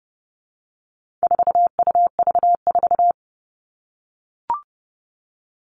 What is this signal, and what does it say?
Morse code, heard as a keyed tone, about 30 wpm, 700 Hz. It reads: 4UV4
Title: All Cut Numbers - 4-Digit 30wpm